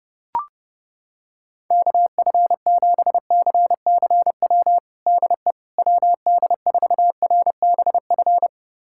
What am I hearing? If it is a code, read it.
Morse code, heard as a keyed tone, about 30 wpm, 700 Hz. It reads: KF7CCW DE WD4RBF
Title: USA Callsign DE Callsign - 6 Characters - 30wpm